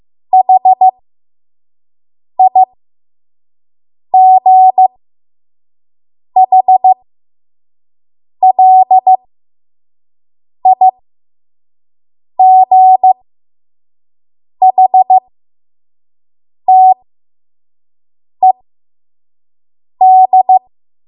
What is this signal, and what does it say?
Morse code, heard as a keyed tone, about 15 wpm, spaced as for 5 wpm, 750 Hz. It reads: HIGHLIGHTED
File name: Morse / 05WPM